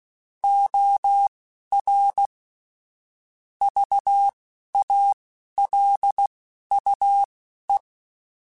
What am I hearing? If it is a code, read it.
Morse code, heard as a keyed tone, about 16 wpm, spaced as for 11 wpm, 800 Hz. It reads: OR VALUE
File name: Technical Subjects / Technical Subjects-mp3s / XC @10